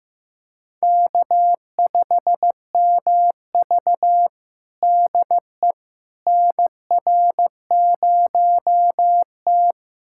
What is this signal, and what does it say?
Morse code, heard as a keyed tone, about 15 wpm, 700 Hz. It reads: K5MV DE NR0T
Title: USA Callsign DE Callsign - 4 Characters - 15wpm